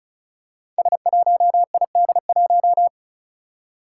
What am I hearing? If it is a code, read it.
Morse code, heard as a keyed tone, about 36 wpm, 700 Hz. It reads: S1ID1
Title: Five Challenging Letter-Number Combinations 35wpm